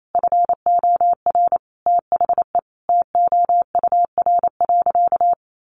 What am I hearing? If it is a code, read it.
Morse code, heard as a keyed tone, about 28 wpm, 700 Hz. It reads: FOR THE TOUR.
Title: Sentences from Top 1400 Words 28wpm